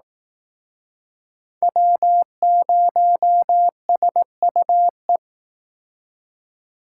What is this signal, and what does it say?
Morse code, heard as a keyed tone, about 19 wpm, 700 Hz. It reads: W0SUE
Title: Speed Racing - USA 1x3 Call Signs 15wpm